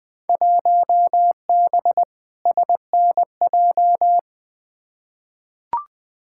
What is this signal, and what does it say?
Morse code, heard as a keyed tone, about 20 wpm, 700 Hz. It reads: 1B SNJ
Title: Field Day Exchanges - 20wpm